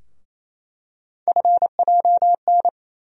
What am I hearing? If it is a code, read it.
Morse code, heard as a keyed tone, about 28 wpm, 700 Hz. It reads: FJN
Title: USA Call Sign Postfixes - 3 Letters 28wpm